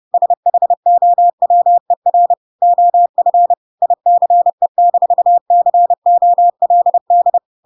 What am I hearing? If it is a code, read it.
Morse code, heard as a keyed tone, about 30 wpm, 700 Hz. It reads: SHOWER OF ICE-COLD
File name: Pinocchio / Pinocchio ch6 30 30 ik1hge